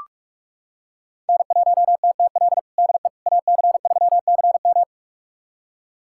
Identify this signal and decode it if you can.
Morse code, heard as a keyed tone, about 41 wpm, 700 Hz. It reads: N1TTL DE AC3CM
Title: USA Callsign DE Callsign - 5 Characters - 45wpm